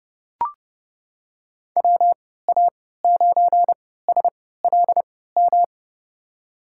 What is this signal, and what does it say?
Morse code, heard as a keyed tone, about 32 wpm, spaced as for 17 wpm, 700 Hz. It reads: WA9SLM
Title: USA Call Signs - 6 Characters - 3x Character Spacing - 30wpm